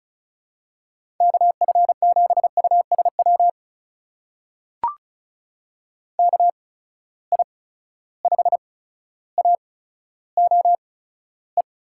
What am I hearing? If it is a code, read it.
Morse code, heard as a keyed tone, about 36 wpm, spaced as for 15 wpm, 700 Hz. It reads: KF7USW KI5AOE
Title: USA Call Signs - 6 Characters - 8x Character Spacing - 35wpm